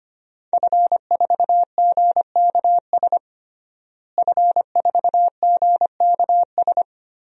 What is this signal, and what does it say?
Morse code, heard as a keyed tone, about 25 wpm, 700 Hz. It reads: F4GKS F4GKS
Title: DXCC-100 Call Signs - 5 Characters - Repeated 2x - 25wpm